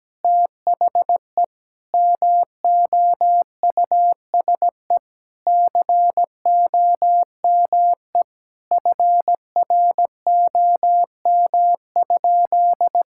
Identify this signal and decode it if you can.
Morse code, heard as a keyed tone, about 17 wpm, 700 Hz. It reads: THE MOUSE COME FROM?
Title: Sentences from Top 1000 Words - Review Part 5 - 17wpm